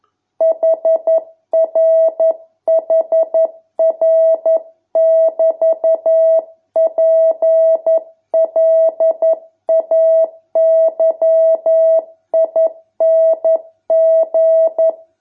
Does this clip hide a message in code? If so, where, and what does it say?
Yes: Morse code from 0.4 to 14.9 s, heard as a keyed tone, about 11 wpm, 650 Hz. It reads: HRHR<BT>PLAYING